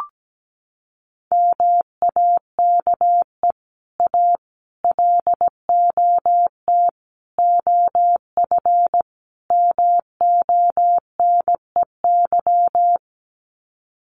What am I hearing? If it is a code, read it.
Morse code, heard as a keyed tone, about 17 wpm, 700 Hz. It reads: MAKE A LOT OF MONEY
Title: Sets of 5 Words Encore2 17wpm